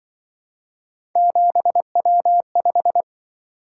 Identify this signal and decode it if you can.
Morse code, heard as a keyed tone, about 24 wpm, 700 Hz. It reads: 7W5
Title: Speed-Racing - Three Letter-Number Combinations - 20wpm